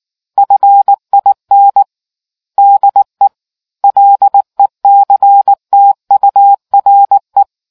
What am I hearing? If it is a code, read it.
Morse code, heard as a keyed tone, about 19 wpm, 800 Hz. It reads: FIN DE LECTURE